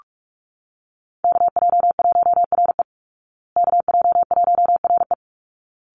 Morse code, heard as a keyed tone, about 44 wpm, 700 Hz. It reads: KJ1RE KJ1RE
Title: Call Signs Repeated 3x - 45wpm